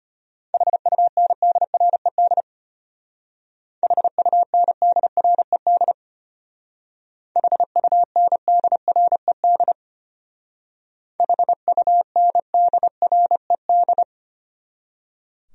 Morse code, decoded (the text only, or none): HUNDRED HUNDRED HUNDRED HUNDRED